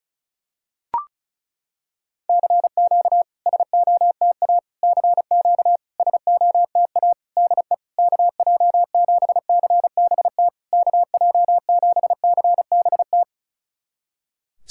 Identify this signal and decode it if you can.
Morse code, heard as a keyed tone, about 36 wpm, 700 Hz. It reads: CQ SOTA CQ SOTA DE KJ7CBT KJ7CBT
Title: SOTA QSO Practice 35wpm